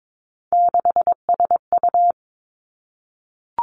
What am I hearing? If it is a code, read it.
Morse code, heard as a keyed tone, about 22 wpm, 700 Hz. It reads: 6SU
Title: Three Challenging Letter-Number Combinations 22wpm